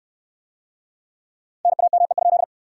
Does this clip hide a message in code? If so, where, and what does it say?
Yes: Morse code from 1.6 to 2.4 s, heard as a keyed tone, about 69 wpm, 700 Hz. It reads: NAME?